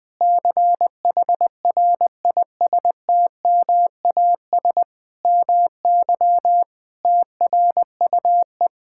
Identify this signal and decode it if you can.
Morse code, heard as a keyed tone, about 20 wpm, 700 Hz. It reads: CHRISTMAS MY TRUE
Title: Sentences with 8-Word Phrases - 20wpm